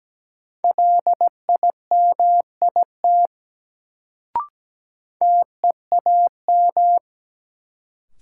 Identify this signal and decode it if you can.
Morse code, heard as a keyed tone, about 17 wpm, 700 Hz. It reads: LIMIT TEAM
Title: Sentences from Top 1500 Words - New Word Drill - 17wpm